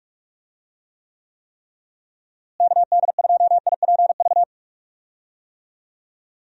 Morse code, heard as a keyed tone, about 46 wpm, 700 Hz. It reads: KD2IPV
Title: USA Call Signs - 6 Characters - Repeated 3x - 45wpm